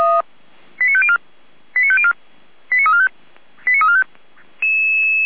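Five-tone selective call (ZVEI), heard as quick runs of five tones, tones of about 70 ms. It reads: 89504 89504 89345 89345
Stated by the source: ZVEI five-tone selective call 89305 received by SDR on channel 455, from feuerwehr-orlamuende.org fire brigade audio archive.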